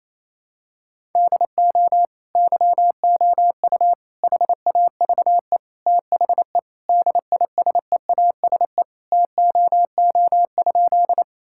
Morse code, heard as a keyed tone, about 27 wpm, 700 Hz. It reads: DO YOU HAVE THE DISEASE TOO?
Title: Sentences from Top 1000 Words - Review Part 6 - 28wpm